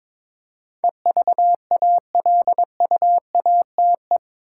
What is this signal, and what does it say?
Morse code, heard as a keyed tone, about 22 wpm, 700 Hz. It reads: EVALUATE
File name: practice / 22wpm / Eight Letter Words - 22wpm